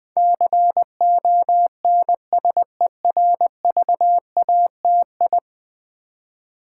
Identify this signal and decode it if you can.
Morse code, heard as a keyed tone, about 20 wpm, 700 Hz. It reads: CONSERVATI
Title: Build A Word - 5 Letters At A Time - 12-Letter Words - 20wpm